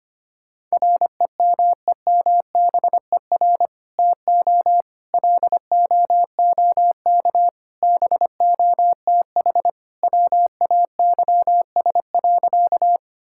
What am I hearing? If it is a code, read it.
Morse code, heard as a keyed tone, about 25 wpm, 700 Hz. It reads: REMEMBER TO LOOK BOTH WAYS.
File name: practice / 25wpm / Sentences with 2-Word Phrases - 25wpm